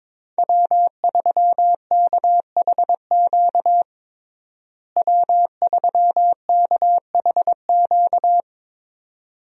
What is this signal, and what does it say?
Morse code, heard as a keyed tone, about 21 wpm, 700 Hz. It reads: W3KHQ W3KHQ
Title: USA Call Signs - 5 Characters - Repeated 2x - 22wpm